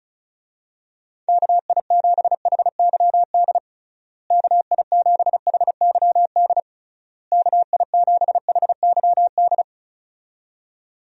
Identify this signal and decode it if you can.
Morse code, heard as a keyed tone, about 34 wpm, 700 Hz. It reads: KI7HYD KI7HYD KI7HYD
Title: USA Call Signs - 6 Characters - Repeated 3x - 35wpm